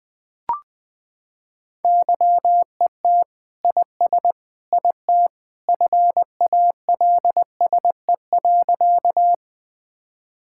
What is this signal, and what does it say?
Morse code, heard as a keyed tone, about 19 wpm, 700 Hz. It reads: YET IS IT FALSE.
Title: Sentences from Top 1600 Words 20wpm